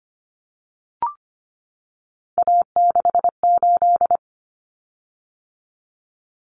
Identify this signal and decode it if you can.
Morse code, heard as a keyed tone, about 25 wpm, 700 Hz. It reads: A68
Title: Common Cut Numbers - 3-Digit 25wpm